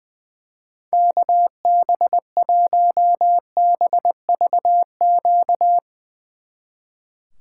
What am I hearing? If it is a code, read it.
Morse code, heard as a keyed tone, about 20 wpm, 700 Hz. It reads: KB1BVQ
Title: Speed Racing - USA 2x3 Call Signs 20wpm